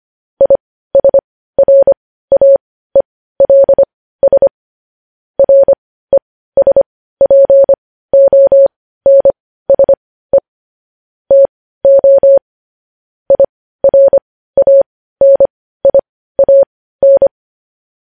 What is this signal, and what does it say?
Morse code, heard as a keyed tone, about 25 wpm, spaced as for 15 wpm, 550 Hz. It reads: ISRAELS RESPONSE TO IRANIAN
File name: podcast / News headlines-1x-25x15